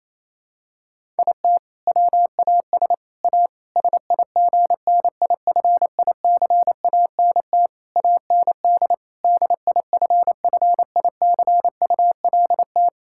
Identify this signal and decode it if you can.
Morse code, heard as a keyed tone, about 28 wpm, 700 Hz. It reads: IT WAS A SIGNIFICANT AND DIFFICULT
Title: Sentences from Top 1000 Words - Review Part 6 - 28wpm